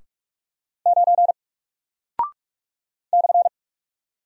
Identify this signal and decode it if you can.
Morse code, heard as a keyed tone, about 44 wpm, 700 Hz. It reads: 9 /
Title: Single Character (Letter:Number:Punctuation:Prosign) Alt 45wpm